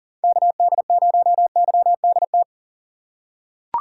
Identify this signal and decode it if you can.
Morse code, heard as a keyed tone, about 40 wpm, 700 Hz. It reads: KD0YDT
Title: USA Call Signs - 6 Characters - 2x Character Spacing - 40wpm